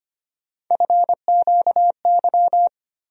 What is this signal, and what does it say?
Morse code, heard as a keyed tone, about 25 wpm, 700 Hz. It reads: FQY
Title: USA Call Sign Postfixes - 3 Letters 25wpm